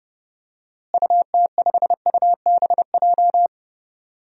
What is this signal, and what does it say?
Morse code, heard as a keyed tone, about 28 wpm, 700 Hz. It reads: UT5UBJ
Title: DXCC-100 Call Signs - 6 Characters - Repeated 3x - 30wpm